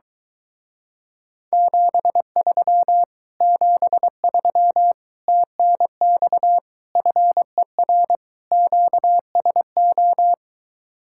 Morse code, heard as a keyed tone, about 22 wpm, 700 Hz. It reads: 73 73 TNX FER QSO